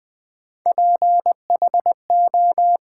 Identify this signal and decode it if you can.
Morse code, heard as a keyed tone, about 20 wpm, 700 Hz. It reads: PHO
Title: Build A Word - 1 Letter At A Time - 3 to 5-Letter Words - 20wpm